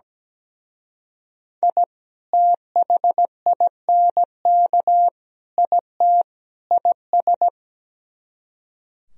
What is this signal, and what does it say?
Morse code, heard as a keyed tone, about 17 wpm, 700 Hz. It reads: I THINK IT IS
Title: Speed-Racing - Sets of 4 Words 17wpm